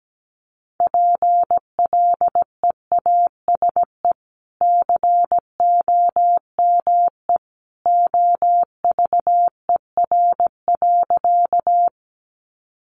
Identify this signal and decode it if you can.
Morse code, heard as a keyed tone, about 17 wpm, 700 Hz. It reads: PLEASE COME OVER.
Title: Sentences from Top 1500 Words 17wpm